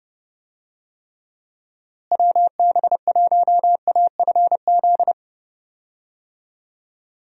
Morse code, heard as a keyed tone, about 32 wpm, 700 Hz. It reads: WB1AFZ
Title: USA Call Signs - 6 Characters - Repeated 3x - 30wpm